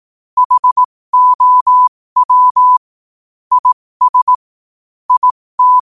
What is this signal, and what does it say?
Morse code, heard as a keyed tone, about 18 wpm, spaced as for 15 wpm, 1000 Hz. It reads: HOW IS IT
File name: qotdfiles / 15wpm